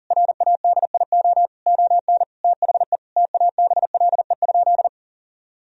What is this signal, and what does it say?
Morse code, heard as a keyed tone, about 40 wpm, 700 Hz. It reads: RADIO ON THE TABLE?
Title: Sentences from Top 1000 Words - Review Part 3 - 40wpm